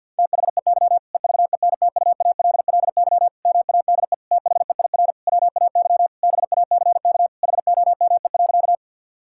Morse code, heard as a keyed tone, about 50 wpm, 700 Hz. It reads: THEY EVENTUALLY MADE THEIR WAY BACK HOME.
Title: Sentences from Top 1000 Words - Review Part 6 - 50wpm